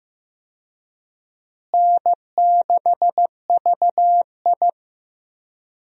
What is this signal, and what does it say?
Morse code, heard as a keyed tone, about 15 wpm, 700 Hz. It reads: N6VI
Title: USA Call Signs - 4 Characters - Repeated 2x - 15wpm